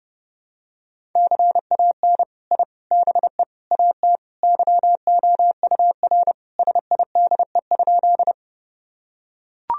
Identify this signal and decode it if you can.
Morse code, heard as a keyed tone, about 32 wpm, 700 Hz. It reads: CAN I BE AT YOUR SIDE?
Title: Sentences from Top 1000 Words - Review Part 1 - 30wpm